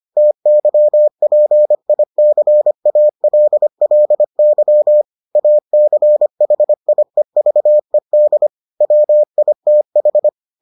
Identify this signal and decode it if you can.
Morse code, heard as a keyed tone, about 25 wpm, 600 Hz. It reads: TYPICALLY ACHIEVED WITH